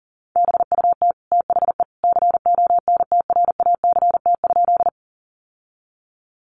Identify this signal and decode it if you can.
Morse code, heard as a keyed tone, about 40 wpm, 700 Hz. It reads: BUT THE CONTRACT?
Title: Sentences from Top 1400 Words 40wpm